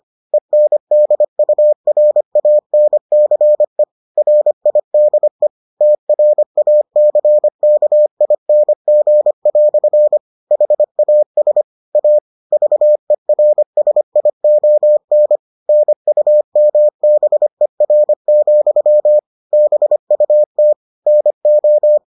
Morse code, heard as a keyed tone, about 25 wpm, 600 Hz. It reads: ENDURANCE RIDE TRACKING" HAS A VERSION NUMBER, BUT NO